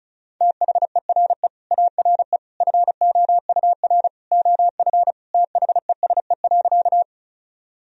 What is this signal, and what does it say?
Morse code, heard as a keyed tone, about 34 wpm, 700 Hz. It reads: THERE ARE FOUR OF THESE.
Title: Sentences from Top 1000 Words - Review Part 1 - 35wpm